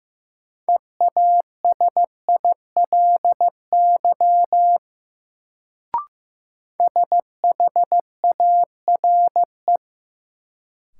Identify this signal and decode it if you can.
Morse code, heard as a keyed tone, about 15 wpm, 700 Hz. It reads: EASILY SHARE
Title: Sentences from Top 900 Words - New Word Drill - 15wpm